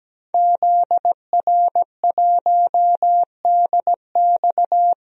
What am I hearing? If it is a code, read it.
Morse code, heard as a keyed tone, about 17 wpm, 700 Hz. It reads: ZR1DX
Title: DXCC-100 Call Signs - 5 Characters - Repeated 3x - 17wpm